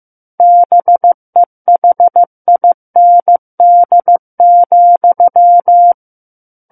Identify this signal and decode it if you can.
Morse code, heard as a keyed tone, about 15 wpm, 700 Hz. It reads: BEHIND,
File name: Book One / 06-Chapter 6 - 15 WPM